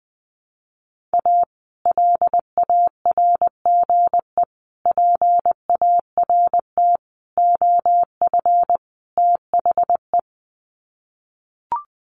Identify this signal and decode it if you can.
Morse code, heard as a keyed tone, about 20 wpm, 700 Hz. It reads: A LARGE PART OF THE
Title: Sets of 5 Words Encore2 20wpm